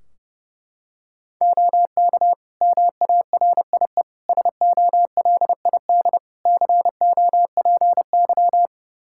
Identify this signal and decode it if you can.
Morse code, heard as a keyed tone, about 32 wpm, 700 Hz. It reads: OK MARIE SOLID COPY
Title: Practice QSO - Exchange 3 - More Info - 30wpm